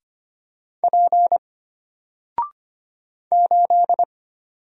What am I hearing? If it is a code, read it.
Morse code, heard as a keyed tone, about 25 wpm, 700 Hz. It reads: P 8
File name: Single-Letter-Number-25wpm